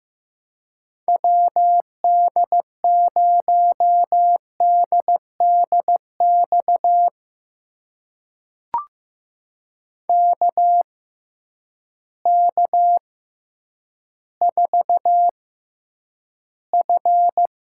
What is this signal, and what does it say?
Morse code, heard as a keyed tone, about 15 wpm, spaced as for 9 wpm, 700 Hz. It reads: WD0DDX KK4F
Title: USA Call Signs - 6 Characters - 6x Character Spacing - 15wpm